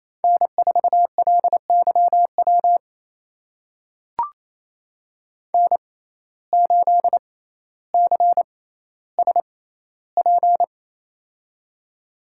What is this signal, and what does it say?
Morse code, heard as a keyed tone, about 28 wpm, spaced as for 14 wpm, 700 Hz. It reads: N4LYW N8CSP
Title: USA Call Signs - 5 Characters - 6x Character Spacing - 28wpm